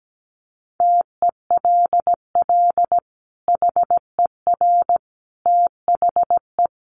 Morse code, heard as a keyed tone, about 17 wpm, 700 Hz. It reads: TELL HER THE
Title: Sentences from Top 800 Words 17wpm